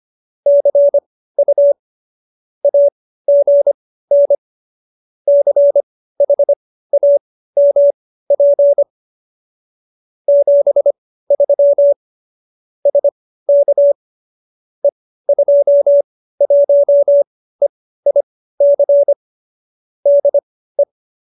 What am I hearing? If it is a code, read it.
Morse code, heard as a keyed tone, about 25 wpm, spaced as for 15 wpm, 570 Hz. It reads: CU AGN CHAMP 73 SK E21EIC DE